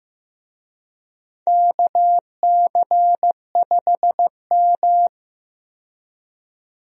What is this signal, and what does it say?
Morse code, heard as a keyed tone, about 16 wpm, 700 Hz. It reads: KC5M
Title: USA Call Signs - 4 Characters - 15wpm